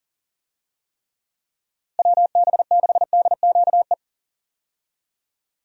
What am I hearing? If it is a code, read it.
Morse code, heard as a keyed tone, about 40 wpm, 700 Hz. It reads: WB6DQE
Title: USA Call Signs - 6 Characters - Repeated 2x - 40wpm